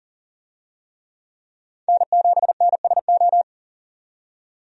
Morse code, heard as a keyed tone, about 40 wpm, 700 Hz. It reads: N7NSO
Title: USA Call Signs - 5 Characters - 40wpm